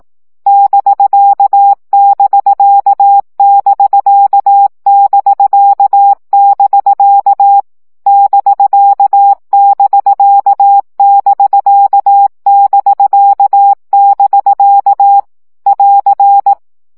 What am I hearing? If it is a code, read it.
Morse code, heard as a keyed tone, about 18 wpm, 800 Hz. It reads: <BK><BK><BK><BK><BK> <BK><BK><BK><BK><BK> <AR>
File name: CW gyakorlas bk 15 baud